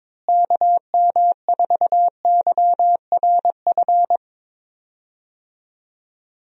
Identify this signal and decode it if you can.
Morse code, heard as a keyed tone, about 21 wpm, 700 Hz. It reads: KM4YRF
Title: USA Call Signs - 6 Characters - Repeated 3x - 22wpm